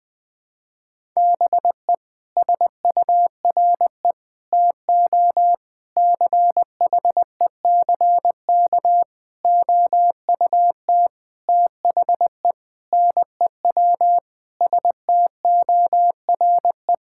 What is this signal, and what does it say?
Morse code, heard as a keyed tone, about 20 wpm, 700 Hz. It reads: BE SURE TO CHECK OUT THE NEW STORE
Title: Sentences with 6-Word Phrases - 20wpm